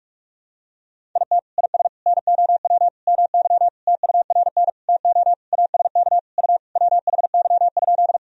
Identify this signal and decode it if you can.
Morse code, heard as a keyed tone, about 46 wpm, 700 Hz. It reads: IT IS NOW MY TURN TO ASK U WHY?
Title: Sentences from Top 200 Words and Abbreviations 45wpm